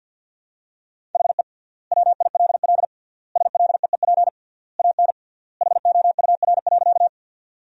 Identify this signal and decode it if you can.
Morse code, heard as a keyed tone, about 51 wpm, 700 Hz. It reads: HE WILL SLEEP AN HOUR.
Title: 3x Word-Spacing - Sentences from Top 500 Words - 50wpm